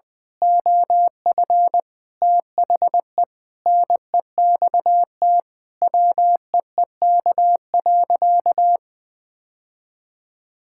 Morse code, heard as a keyed tone, about 20 wpm, 700 Hz. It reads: OF THE NEXT WEEK.
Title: Sentences with 6-Word Phrases - 20wpm